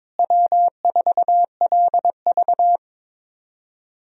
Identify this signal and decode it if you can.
Morse code, heard as a keyed tone, about 23 wpm, 700 Hz. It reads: W4LV
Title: USA Call Signs - 4 Characters - Repeated 2x - 22wpm